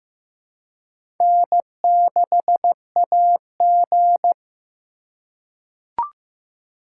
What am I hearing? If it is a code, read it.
Morse code, heard as a keyed tone, about 15 wpm, 700 Hz. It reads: N6AG
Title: USA Call Signs - 4 Characters - 4x Character Spacing - 15wpm